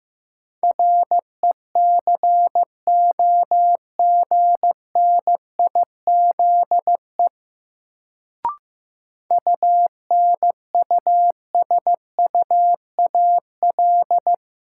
Morse code, heard as a keyed tone, about 15 wpm, 700 Hz. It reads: RECOGNIZE UNUSUAL
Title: Sentences from Top 1400 Words - New Word Drill - 15wpm